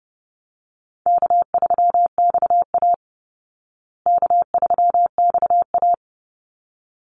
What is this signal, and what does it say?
Morse code, heard as a keyed tone, about 30 wpm, 700 Hz. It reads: K3XA K3XA
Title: Easy Call Signs Repeated 2x - 30wpm